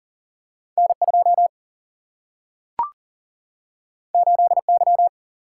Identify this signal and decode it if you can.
Morse code, heard as a keyed tone, about 42 wpm, 700 Hz. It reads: N2 8Y
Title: Single-Letter Single-Number 40wpm